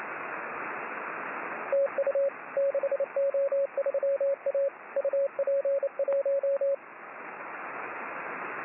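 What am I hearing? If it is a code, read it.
Morse code, heard as a keyed tone, about 28 wpm, 550 Hz, with radio noise. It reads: TU 6O3A UP1
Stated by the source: Morse signal on the 6 m band.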